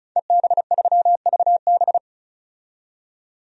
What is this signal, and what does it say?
Morse code, heard as a keyed tone, about 34 wpm, 700 Hz. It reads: EB3VB
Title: DXCC-100 Call Signs - 5 Characters - 35wpm